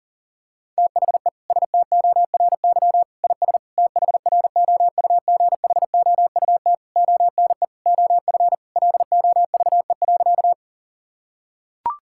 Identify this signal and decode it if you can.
Morse code, heard as a keyed tone, about 40 wpm, 690 Hz. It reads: THE STORY IS THROUGHOUT ONE OF LOVE.
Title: Sentences from Top 1000 Words - Review Part 5 - 40wpm